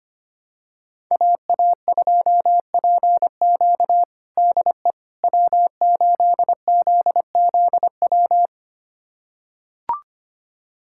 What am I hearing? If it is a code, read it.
Morse code, heard as a keyed tone, about 25 wpm, 700 Hz. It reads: AA2PQ DE W8ZZW
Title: USA Callsign DE Callsign 25wpm